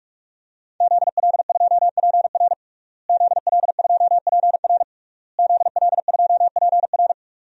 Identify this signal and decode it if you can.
Morse code, heard as a keyed tone, about 45 wpm, 700 Hz. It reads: ZL2PR ZL2PR ZL2PR
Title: DXCC-100 Call Signs - 5 Characters - Repeated 3x - 45wpm